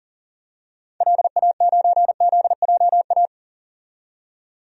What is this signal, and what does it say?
Morse code, heard as a keyed tone, about 40 wpm, 700 Hz. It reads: LA9ZJA
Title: DXCC-100 Call Signs - 6 Characters - 40wpm